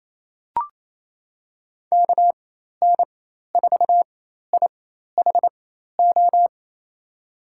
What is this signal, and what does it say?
Morse code, heard as a keyed tone, about 28 wpm, spaced as for 13 wpm, 700 Hz. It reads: KN4IHO
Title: USA Call Signs - 6 Characters - 4x Character Spacing - 28wpm